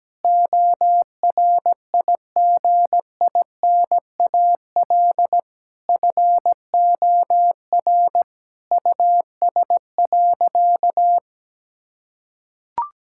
Morse code, heard as a keyed tone, about 17 wpm, 700 Hz. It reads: ORIGINAL FOR US.
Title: Sentences from Top 1000 Words - Review Part 6 - 17wpm